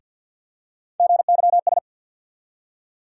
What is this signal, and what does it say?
Morse code, heard as a keyed tone, about 49 wpm, 700 Hz. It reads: GYS
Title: USA Call Sign Postfixes - 3 Letters 50wpm